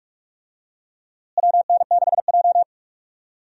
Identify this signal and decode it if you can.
Morse code, heard as a keyed tone, about 47 wpm, 700 Hz. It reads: WN6J
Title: USA Call Signs - 4 Characters - Repeated 2x - 45wpm